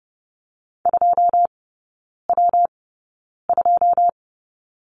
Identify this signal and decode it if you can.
Morse code, heard as a keyed tone, about 30 wpm, spaced as for 9 wpm, 700 Hz. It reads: 2W2
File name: Lesson 19e - Letter w - Three Characters 30wpm